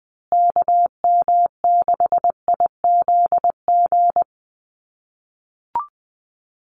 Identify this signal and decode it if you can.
Morse code, heard as a keyed tone, about 20 wpm, 700 Hz. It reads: KM6IZG
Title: USA Call Signs Repeated 3x 20wpm